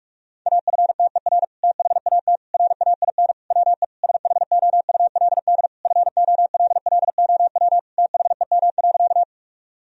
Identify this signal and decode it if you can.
Morse code, heard as a keyed tone, about 43 wpm, 700 Hz. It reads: AFTER THAT RAIN WE SHOULD FOLLOW THEM.
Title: Sentences from Top 1000 Words - Review Part 1 - 45wpm